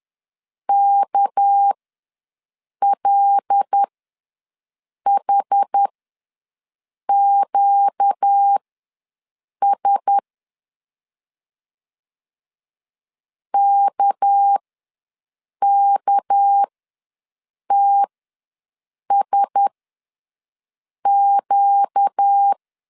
Morse code, heard as a keyed tone, about 10 wpm, spaced as for 6 wpm, 800 Hz. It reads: KLHQS KKTSQ
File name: LEZIONI / 6 PKX0 / 6 PK ws 10wpm 3745